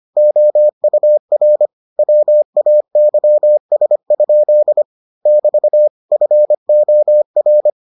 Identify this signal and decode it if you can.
Morse code, heard as a keyed tone, about 25 wpm, 600 Hz. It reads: OUR WAYS? <BT> FOR